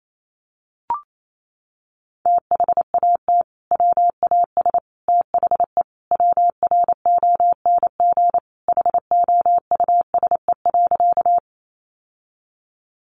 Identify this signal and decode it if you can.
Morse code, heard as a keyed tone, about 28 wpm, 700 Hz. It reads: THAT WAS THE WRONG HOUSE.
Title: Sentences from Top 900 Words 28wpm